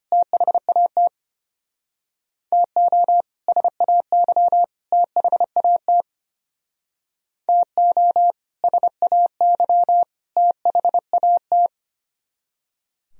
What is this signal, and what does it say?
Morse code, heard as a keyed tone, about 32 wpm, 700 Hz. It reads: THAT TO SAY THAT TO SAY THAT